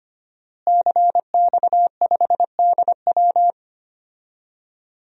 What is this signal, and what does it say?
Morse code, heard as a keyed tone, about 25 wpm, 700 Hz. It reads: CX5DW
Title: DXCC-100 Call Signs - 5 Characters - Repeated 3x - 25wpm